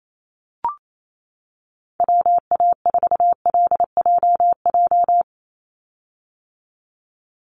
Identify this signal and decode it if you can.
Morse code, heard as a keyed tone, about 28 wpm, 700 Hz. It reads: WA4LJJ
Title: Call Signs - 28wpm